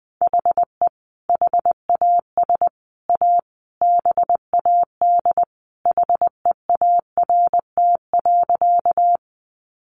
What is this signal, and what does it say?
Morse code, heard as a keyed tone, about 19 wpm, 700 Hz. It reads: HE HAS A BAD HEART.
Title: Sentences from Top 1200 Words 20wpm